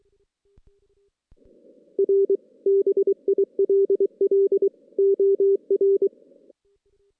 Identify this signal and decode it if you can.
Morse code, heard as a keyed tone, about 23 wpm, 390 Hz. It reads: R BILL OR
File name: 2016 JAN CW NAQP / 2016 Jan naqp k5ot - 0-6